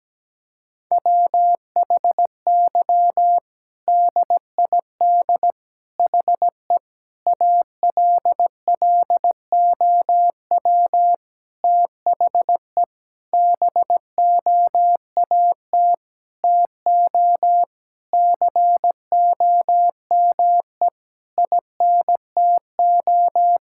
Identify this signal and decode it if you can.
Morse code, heard as a keyed tone, about 17 wpm, 700 Hz. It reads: WHY DID HE ALLOW THE BOAT TO COME INTO
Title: Sentences from Top 1000 Words - Review Part 5 - 17wpm